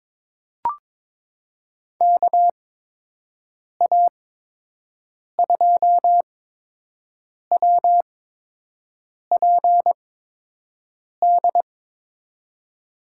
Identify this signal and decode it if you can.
Morse code, heard as a keyed tone, about 22 wpm, spaced as for 6 wpm, 700 Hz. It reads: KA2WPD
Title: USA Call Signs - 6 Characters - 8x Character Spacing - 22wpm